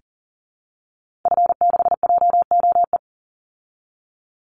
Morse code, heard as a keyed tone, about 40 wpm, 700 Hz. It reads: F6JOE
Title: Call Signs Repeated 3x - 40wpm